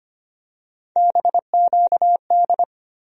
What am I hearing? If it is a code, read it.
Morse code, heard as a keyed tone, about 25 wpm, 700 Hz. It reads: BQD
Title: USA Call Sign Postfixes - 3 Letters 25wpm